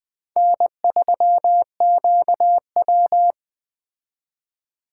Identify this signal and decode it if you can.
Morse code, heard as a keyed tone, about 20 wpm, 700 Hz. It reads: N3QW
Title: Speed Racing - USA 1x2 Call Signs 17wpm